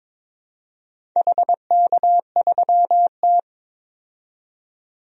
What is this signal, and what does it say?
Morse code, heard as a keyed tone, about 22 wpm, 700 Hz. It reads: HK3T
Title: DXCC-100 Call Signs - 4 Characters - 22wpm